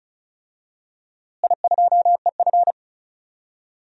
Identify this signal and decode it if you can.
Morse code, heard as a keyed tone, about 34 wpm, 700 Hz. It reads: I2EF
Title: DXCC-100 Call Signs - 4 Characters - Repeated 3x - 35wpm